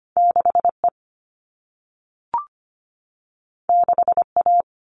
Morse code, heard as a keyed tone, about 25 wpm, 700 Hz. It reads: 6E 6A
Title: All Cut Numbers - 2-Digit 25wpm